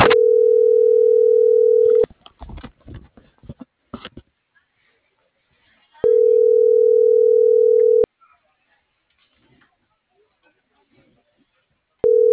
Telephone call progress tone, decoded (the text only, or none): ringback tone